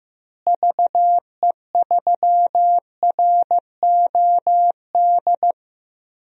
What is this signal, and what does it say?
Morse code, heard as a keyed tone, about 15 wpm, 700 Hz. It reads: VE3ROD
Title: DXCC-100 Call Signs - 6 Characters - Repeated 3x - 15wpm